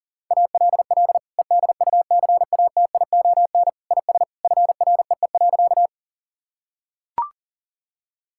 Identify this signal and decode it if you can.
Morse code, heard as a keyed tone, about 40 wpm, 700 Hz. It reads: ALL EDUCATION IS FREE.
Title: Sentences from Top 1000 Words - Review Part 6 - 40wpm